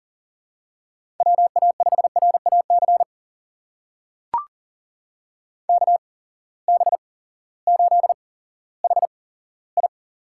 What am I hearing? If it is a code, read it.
Morse code, heard as a keyed tone, about 40 wpm, spaced as for 18 wpm, 700 Hz. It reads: WA5RAC KB8HI
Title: USA Call Signs - 6 Characters - 8x Character Spacing - 40wpm